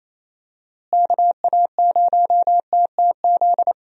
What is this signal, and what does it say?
Morse code, heard as a keyed tone, about 28 wpm, 700 Hz. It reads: KA0TTZ
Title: USA Call Signs - 6 Characters - Repeated 2x - 28wpm